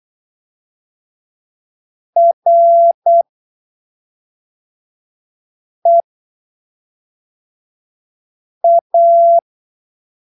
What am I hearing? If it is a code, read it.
Morse code, heard as a keyed tone, about 8 wpm, spaced as for 3 wpm, 680 Hz. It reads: REA
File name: arquivos / texto / texto 07 08ppm